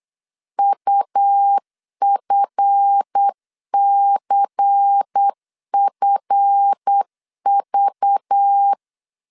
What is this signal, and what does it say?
Morse code, heard as a keyed tone, about 8 wpm, 800 Hz. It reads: UFCFV
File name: LEZIONI / 7 GF9Z / 7 GF 8wpm 3108